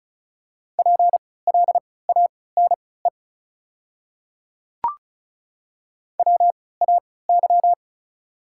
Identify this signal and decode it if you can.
Morse code, heard as a keyed tone, about 34 wpm, spaced as for 20 wpm, 690 Hz. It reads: PLANE WAY
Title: 3x Character Spacing - Top 500 Words - 35wpm